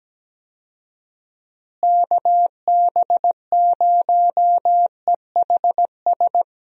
Morse code, heard as a keyed tone, about 17 wpm, 700 Hz. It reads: KB0EHS
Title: USA Call Signs - 6 Characters - Repeated 2x - 17wpm